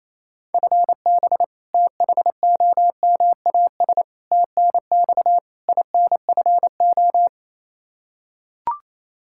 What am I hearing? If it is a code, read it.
Morse code, heard as a keyed tone, about 27 wpm, 700 Hz. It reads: FB THOMAS TNX INFO
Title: Practice QSO - Exchange 3 - More Info - 28wpm